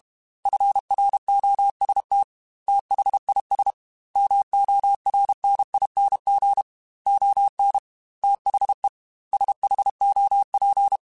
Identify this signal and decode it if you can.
Morse code, heard as a keyed tone, about 32 wpm, 800 Hz. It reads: FROST THIS MORNING ON THE SHOP
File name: Weather Sets / Weather Sets-mp3s / Weather 4 24 @30